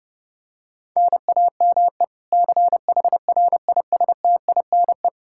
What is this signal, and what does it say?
Morse code, heard as a keyed tone, about 30 wpm, 700 Hz. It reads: NAME CHRISTINE
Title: Practice QSO - Exchange 1 - Beginning - 30wpm